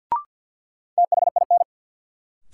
Morse code, heard as a keyed tone, about 52 wpm, 700 Hz. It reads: THIN
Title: Sentences from Top 1200 Words - New Word Drill Alt - 50wpm